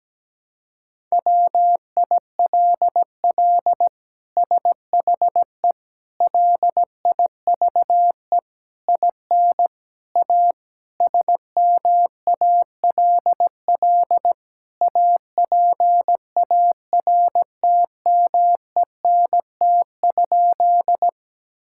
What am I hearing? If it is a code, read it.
Morse code, heard as a keyed tone, about 17 wpm, 700 Hz. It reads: WILL SHE LIVE IN A SMALL APARTMENT?
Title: Sentences from Top 1000 Words - Review Part 6 - 17wpm